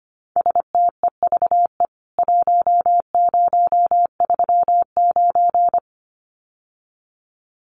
Morse code, heard as a keyed tone, about 25 wpm, 700 Hz. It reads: STEVE 1039
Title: CWT Exchange 25wpm